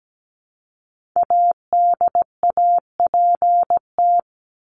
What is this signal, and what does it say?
Morse code, heard as a keyed tone, about 17 wpm, 700 Hz. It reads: ADAPT
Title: Top 1501-1750 Words 17wpm